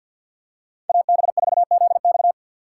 Morse code, heard as a keyed tone, about 48 wpm, 700 Hz. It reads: AB4ZX
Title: USA Call Signs - 5 Characters - 2x Character Spacing - 50wpm